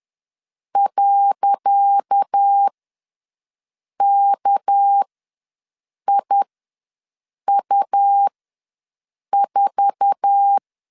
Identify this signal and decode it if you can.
Morse code, heard as a keyed tone, about 10 wpm, spaced as for 6 wpm, 800 Hz. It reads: .KIU4
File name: LEZIONI / 10 SIGNS / 10C DOT ws 10wpm 4213